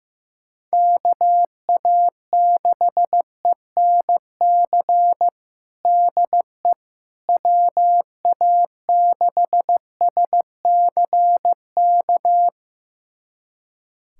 Morse code, heard as a keyed tone, about 15 wpm, 700 Hz. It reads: KA6ENC DE WA6SCK
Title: USA Callsign DE Callsign - 6 Characters - 15wpm